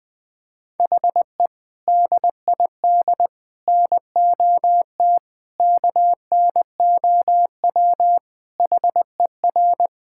Morse code, heard as a keyed tone, about 20 wpm, 700 Hz. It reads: HE DID NOT KNOW HER
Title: Sentences with 4-Word Phrases - 20wpm